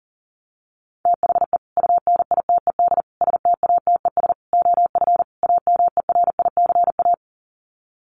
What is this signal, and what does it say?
Morse code, heard as a keyed tone, about 40 wpm, 700 Hz. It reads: THE UNITED STATES OF AMERICA
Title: Sets of 5 Words 40wpm